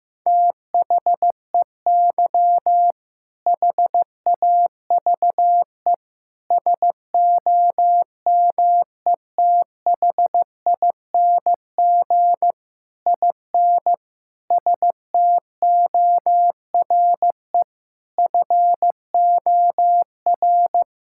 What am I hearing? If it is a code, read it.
Morse code, heard as a keyed tone, about 16 wpm, 700 Hz. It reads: THEY HAVE SOMETHING IN STORE FOR
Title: Sentences from Top 1000 Words - Review Part 5 - 15wpm